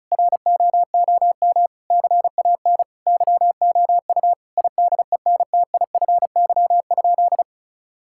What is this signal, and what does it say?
Morse code, heard as a keyed tone, about 36 wpm, 700 Hz. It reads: ROOM CAN YOU IDENTIFY?
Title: Sentences from Top 1000 Words - Review Part 6 - 35wpm